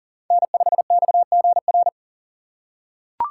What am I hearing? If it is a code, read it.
Morse code, heard as a keyed tone, about 40 wpm, 700 Hz. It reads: N5XGR
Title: USA Call Signs - 5 Characters - 4x Character Spacing - 40wpm